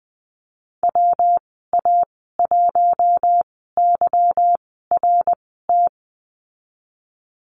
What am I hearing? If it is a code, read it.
Morse code, heard as a keyed tone, about 20 wpm, spaced as for 15 wpm, 700 Hz. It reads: WA1YRT
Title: 154 Lesson 25g - Letter p - Call Signs 20wpm